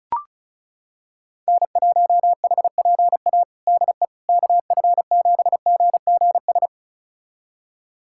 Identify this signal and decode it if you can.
Morse code, heard as a keyed tone, about 34 wpm, 700 Hz. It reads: N1HPA DE KF7GGS
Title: USA Callsign DE Callsign 35wpm